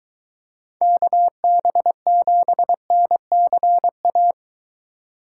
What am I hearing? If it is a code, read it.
Morse code, heard as a keyed tone, about 23 wpm, 700 Hz. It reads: KB7NCA